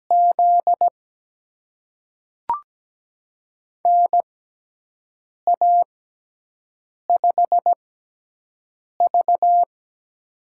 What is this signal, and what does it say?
Morse code, heard as a keyed tone, about 17 wpm, spaced as for 6 wpm, 700 Hz. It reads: Z NA5V
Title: USA Call Signs - 6x Character Spacing - 17wpm